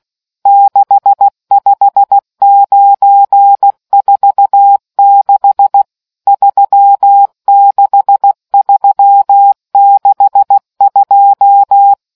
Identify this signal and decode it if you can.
Morse code, heard as a keyed tone, about 16 wpm, 800 Hz. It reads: 65946 36362